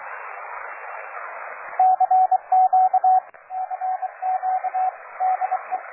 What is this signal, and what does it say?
Morse code, heard as a keyed tone, about 26 wpm, 750 Hz, with radio noise. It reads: CQ CQ DE
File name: cw forum / OJ0B sur 1823 le 21-09-2008 17h00 test GP XXL